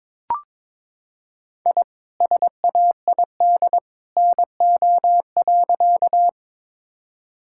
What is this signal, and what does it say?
Morse code, heard as a keyed tone, about 22 wpm, 700 Hz. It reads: I SAID NO.
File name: practice / 22wpm / Sentences with 2-Word Phrases - 22wpm